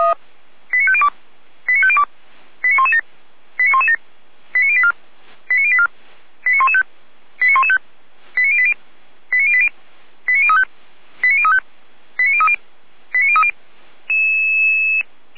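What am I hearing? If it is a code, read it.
Five-tone selective call (ZVEI), heard as quick runs of five tones, tones of about 70 ms. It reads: 89502 89502 89117 89117 89084 89084 89115 89115 89080 89080 89035 89035 89030 89030
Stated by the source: ZVEI five-tone selective call received with an SDR.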